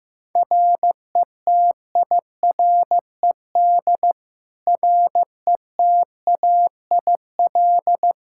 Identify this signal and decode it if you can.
Morse code, heard as a keyed tone, about 15 wpm, 700 Hz. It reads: RETIRED RETAIL
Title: Practice QSO - Exchange 3 - More Info - 15wpm